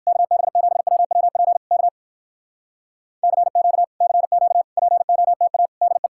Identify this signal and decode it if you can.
Morse code, heard as a keyed tone, about 59 wpm, 700 Hz. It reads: KB7WWP K CQ CQ POTA DE